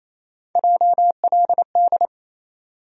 Morse code, heard as a keyed tone, about 28 wpm, 700 Hz. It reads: JLD
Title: USA Call Sign Postfixes - 3 Letters 28wpm